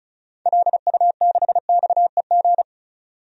Morse code, heard as a keyed tone, about 34 wpm, 700 Hz. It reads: LU6XEG
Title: DXCC-100 Call Signs - 6 Characters - 35wpm